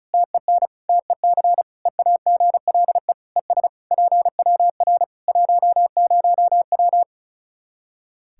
Morse code, heard as a keyed tone, about 36 wpm, 700 Hz. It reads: TEN TEC EAGLE ES PWR 10W
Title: K1USN SST QSO Practice 35wpm